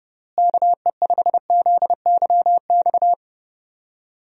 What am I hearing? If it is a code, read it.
Morse code, heard as a keyed tone, about 32 wpm, 700 Hz. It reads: KE5ZYX
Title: USA Call Signs - 6 Characters - Repeated 2x - 30wpm